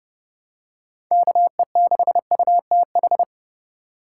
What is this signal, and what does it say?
Morse code, heard as a keyed tone, about 30 wpm, 700 Hz. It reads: KE6UTH
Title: USA Call Signs - 6 Characters - 3x Character Spacing - 30wpm